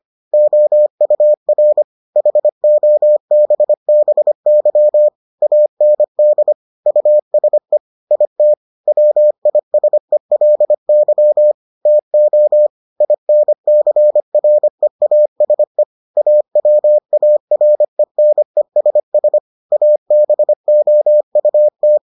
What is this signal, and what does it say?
Morse code, heard as a keyed tone, about 25 wpm, 600 Hz. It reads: OUR HOBBY AND USE IT WISELY TO INCREASE AWARENESS ABOUT